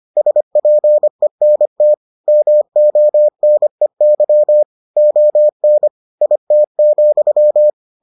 Morse code, heard as a keyed tone, about 25 wpm, 600 Hz. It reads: SPENT MONEY ON IT,